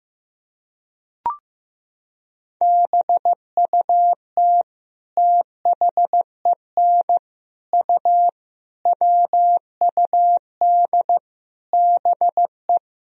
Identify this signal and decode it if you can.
Morse code, heard as a keyed tone, about 15 wpm, 700 Hz. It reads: BUT THEN U WUD BE
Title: Sentences from Top 300 Words and Abbreviations 15wpm